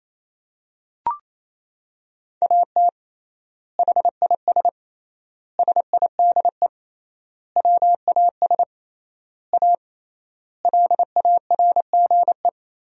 Morse code, heard as a keyed tone, about 28 wpm, 700 Hz. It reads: AT HIS SIDE WAS A LARGE
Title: 3x Word-Spacing - Sentences from Top 500 Words - 28wpm